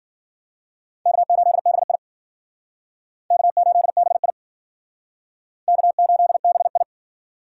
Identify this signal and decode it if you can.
Morse code, heard as a keyed tone, about 52 wpm, 700 Hz. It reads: K8BI K8BI K8BI